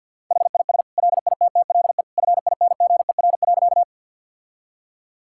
Morse code, heard as a keyed tone, about 50 wpm, 700 Hz. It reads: HIS LITTLE FINGER.
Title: Sentences from Top 1600 Words 50wpm